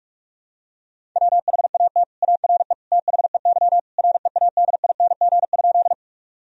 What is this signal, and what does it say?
Morse code, heard as a keyed tone, about 41 wpm, 700 Hz. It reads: WHAT ARE THEY READING?
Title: Sentences from Top 1000 Words - Review Part 3 - 45wpm